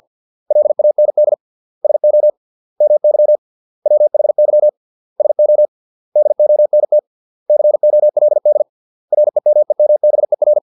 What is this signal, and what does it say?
Morse code, heard as a keyed tone, about 50 wpm, 600 Hz. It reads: LAND SO MY WHY SO DONT COLD REMEMBER